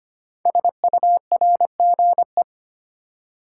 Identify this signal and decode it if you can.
Morse code, heard as a keyed tone, about 25 wpm, 700 Hz. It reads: SURGE
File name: 5 Letter Words - First 3 Letters Given - 25wpm